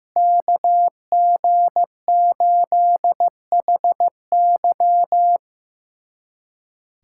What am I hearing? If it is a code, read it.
Morse code, heard as a keyed tone, about 15 wpm, 700 Hz. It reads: KG8HY
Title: USA Callsign DE Callsign 15wpm